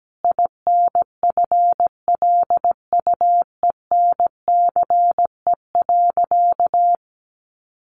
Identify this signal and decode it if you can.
Morse code, heard as a keyed tone, about 17 wpm, 700 Hz. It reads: INFLUENCE.
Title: Sentences from Top 1100 Words 17wpm